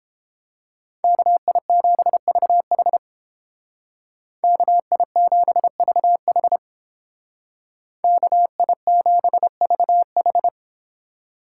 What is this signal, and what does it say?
Morse code, heard as a keyed tone, about 29 wpm, 700 Hz. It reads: KI7VH KI7VH KI7VH